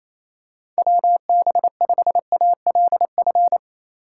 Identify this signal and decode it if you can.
Morse code, heard as a keyed tone, about 27 wpm, 700 Hz. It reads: WB5ALF
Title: USA Call Signs - 6 Characters - 4x Character Spacing - 28wpm